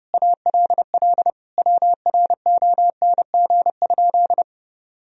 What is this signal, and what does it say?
Morse code, heard as a keyed tone, about 32 wpm, 700 Hz. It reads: ALL WRONG?
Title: Sentences from Top 1000 Words - Review Part 1 - 30wpm